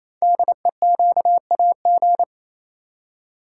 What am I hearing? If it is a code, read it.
Morse code, heard as a keyed tone, about 27 wpm, 700 Hz. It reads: DEQAG
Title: Code Groups - 5 Letters - 28wpm